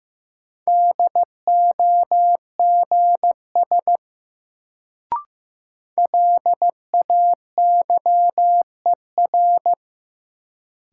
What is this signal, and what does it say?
Morse code, heard as a keyed tone, about 15 wpm, 700 Hz. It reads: DOGS LAYER
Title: Sentences from Top 1600 Words - New Word Drill - 15wpm